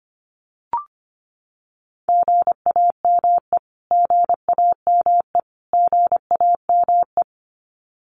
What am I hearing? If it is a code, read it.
Morse code, heard as a keyed tone, about 25 wpm, 700 Hz. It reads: GAME GAME GAME
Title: Top 500 Words Repeated 25wpm